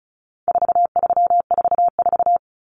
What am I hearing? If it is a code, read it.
Morse code, heard as a keyed tone, about 34 wpm, 700 Hz. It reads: 4344
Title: Common Cut Numbers - 4-Digit 35wpm